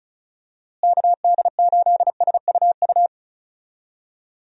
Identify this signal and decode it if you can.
Morse code, heard as a keyed tone, about 34 wpm, 700 Hz. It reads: KD8SUU
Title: USA Call Signs - 6 Characters - Repeated 2x - 35wpm